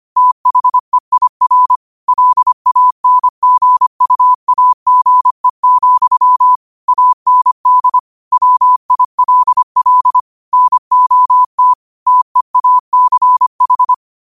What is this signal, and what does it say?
Morse code, heard as a keyed tone, about 25 wpm, 1000 Hz. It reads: THEIR LANGUAGE, AND WILL NOT TEACH
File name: qotdfiles / 25wpm